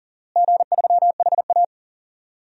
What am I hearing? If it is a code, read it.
Morse code, heard as a keyed tone, about 40 wpm, 700 Hz. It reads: G3HA
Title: DXCC-100 Call Signs - 4 Characters - Repeated 3x - 40wpm